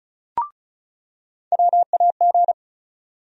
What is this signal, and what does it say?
Morse code, heard as a keyed tone, about 36 wpm, 700 Hz. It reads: WAG
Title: USA Call Sign Postfixes - 3 Letters 35wpm